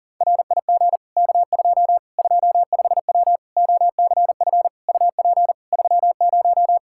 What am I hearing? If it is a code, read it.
Morse code, heard as a keyed tone, about 40 wpm, 700 Hz. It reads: RIG K2 25W OCF UP 30
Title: Practice QSO - Exchange 2 - Info - 40wpm